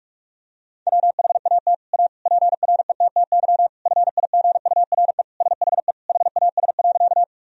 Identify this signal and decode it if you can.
Morse code, heard as a keyed tone, about 41 wpm, 700 Hz. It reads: WHAT A PRETTY FIGURE SHE HAS.
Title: Sentences from Top 1000 Words - Review Part 2 - 45wpm